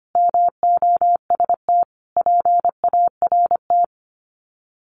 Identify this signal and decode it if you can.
Morse code, heard as a keyed tone, about 25 wpm, 700 Hz. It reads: MOST PART
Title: Sets of 5 Words Encore 25wpm